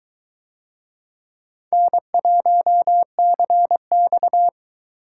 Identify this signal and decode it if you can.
Morse code, heard as a keyed tone, about 20 wpm, 700 Hz. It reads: N1CX